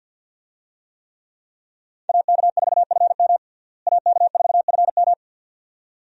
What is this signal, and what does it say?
Morse code, heard as a keyed tone, about 48 wpm, 700 Hz. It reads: AK4FM AK4FM
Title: USA Call Signs - 5 Characters - Repeated 2x - 50wpm